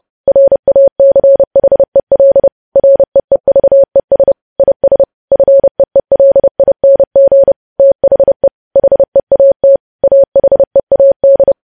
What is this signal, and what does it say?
Morse code, heard as a keyed tone, about 30 wpm, 550 Hz. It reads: RACHEL REEVES IS FEELING THE HEAT AHEAD
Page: news headlines in morse code at 30 wpm